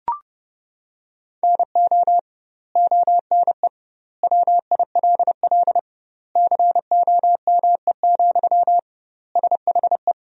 Morse code, heard as a keyed tone, about 30 wpm, 700 Hz. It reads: NO ONE WILL COME, SHE
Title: 2x Word-Spacing - Sentences from Top 100 Words - 30wpm